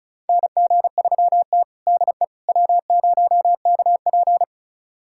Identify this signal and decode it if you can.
Morse code, heard as a keyed tone, about 34 wpm, 700 Hz. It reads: NG3T DE W0KP
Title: USA Callsign DE Callsign - 4 Characters - 35wpm